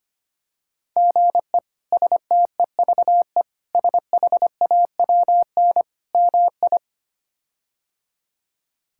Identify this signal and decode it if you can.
Morse code, heard as a keyed tone, about 25 wpm, 700 Hz. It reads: GE STEVE SHAWN MI
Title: K1USN SST QSO Practice - 25wpm